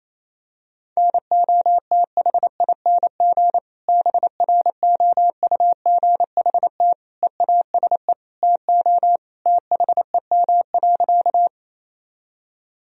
Morse code, heard as a keyed tone, about 28 wpm, 700 Hz. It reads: NOTHING BROUGHT EASE TO THEM.
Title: Sentences from Top 1000 Words - Review Part 1 - 28wpm